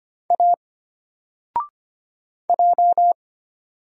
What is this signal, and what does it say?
Morse code, heard as a keyed tone, about 25 wpm, 700 Hz. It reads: A J
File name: Single-Letter-Number-25wpm